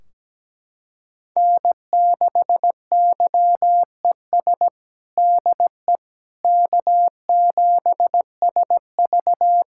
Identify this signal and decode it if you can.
Morse code, heard as a keyed tone, about 17 wpm, 700 Hz. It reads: N6YES DE K7SV
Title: K1USN SST QSO Practice 17wpm